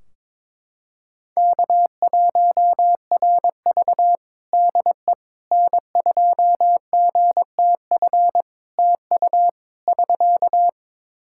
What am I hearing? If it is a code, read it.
Morse code, heard as a keyed tone, about 22 wpm, 700 Hz. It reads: K1RV DE N2GTF TU <SK>
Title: Practice QSO - Exchange 4 - Ending - 22wpm